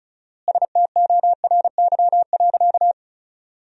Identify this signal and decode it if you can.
Morse code, heard as a keyed tone, about 36 wpm, 700 Hz. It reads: STORY.
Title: 4x Word-Spacing - Sentences from Top 300 Words - 35wpm